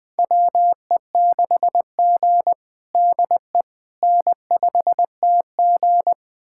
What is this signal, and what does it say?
Morse code, heard as a keyed tone, about 19 wpm, 700 Hz. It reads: WE6G DE N5TG
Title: USA Callsign DE Callsign - 4 Characters - 20wpm